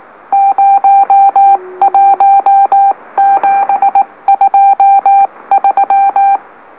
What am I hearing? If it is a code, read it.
Morse code, heard as a keyed tone, about 19 wpm, spaced as for 17 wpm, 790 Hz. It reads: 01723
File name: M04- -2345utc-LOLO